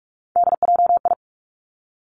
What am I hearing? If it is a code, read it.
Morse code, heard as a keyed tone, about 46 wpm, 700 Hz. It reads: DJI
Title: Three Challenging Letter-Number Combinations 45wpm